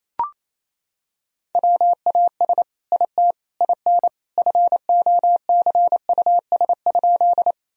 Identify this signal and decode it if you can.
Morse code, heard as a keyed tone, about 28 wpm, 700 Hz. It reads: WAS IT IN FOCUS?
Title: Sentences from Top 1000 Words - Review Part 4 - 28wpm